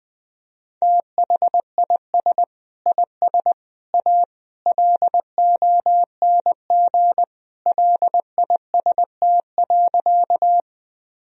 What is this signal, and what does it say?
Morse code, heard as a keyed tone, about 21 wpm, 700 Hz. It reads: THIS IS A LONG LIST.
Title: Sentences from Top 1000 Words - Review Part 2 - 20wpm